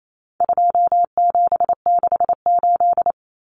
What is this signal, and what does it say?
Morse code, heard as a keyed tone, about 28 wpm, 700 Hz. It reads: 2768
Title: Common Cut Numbers - 4-Digit 28wpm